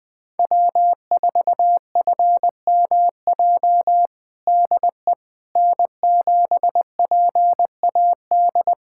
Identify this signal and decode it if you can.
Morse code, heard as a keyed tone, about 21 wpm, 700 Hz. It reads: W4FMJ DE N7PAD
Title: USA Callsign DE Callsign - 5 Characters - 20wpm